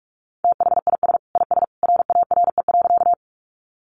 Morse code, heard as a keyed tone, about 44 wpm, 700 Hz. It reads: THIS IS RARE.
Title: Sentences from Top 1100 Words 45wpm